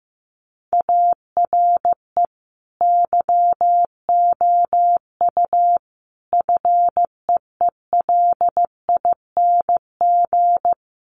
Morse code, heard as a keyed tone, about 14 wpm, 700 Hz. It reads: ARE YOU FEELING
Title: Sentences from Top 1200 Words 15wpm